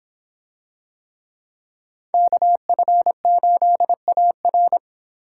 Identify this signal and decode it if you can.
Morse code, heard as a keyed tone, about 25 wpm, 700 Hz. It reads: KF8AR